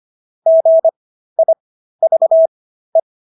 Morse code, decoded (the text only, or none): GIVE